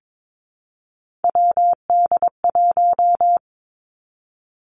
Morse code, heard as a keyed tone, about 22 wpm, 700 Hz. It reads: WD1
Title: Three Challenging Letter-Number Combinations 22wpm